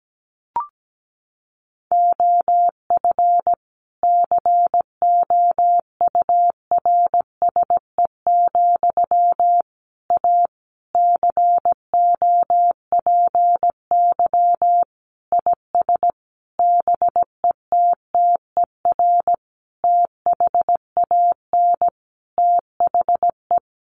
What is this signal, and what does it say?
Morse code, heard as a keyed tone, about 17 wpm, 700 Hz. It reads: OF COURSE, A COPY IS BETTER THAN THE
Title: Sentences from Top 1400 Words 17wpm